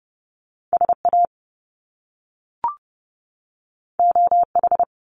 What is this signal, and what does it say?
Morse code, heard as a keyed tone, about 30 wpm, 700 Hz. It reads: SA OH
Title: Top Two Letter Combinations in Words 30wpm